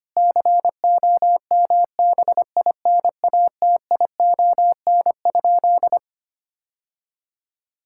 Morse code, heard as a keyed tone, about 25 wpm, 700 Hz. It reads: COMBINATION?
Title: Sentences from Top 1000 Words - Review Part 6 - 25wpm